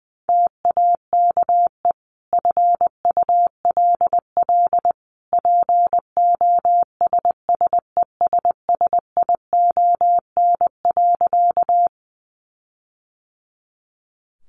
Morse code, heard as a keyed tone, about 20 wpm, 700 Hz. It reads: TAKE FULL POSSESSION.
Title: Sentences from Top 1400 Words 20wpm